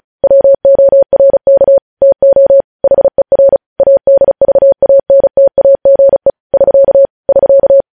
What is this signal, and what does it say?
Morse code, heard as a keyed tone, about 36 wpm, 550 Hz. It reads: WORK TO HER ADVANTAGE <SK> <SK>
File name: podcast / News headlines-1x-35x35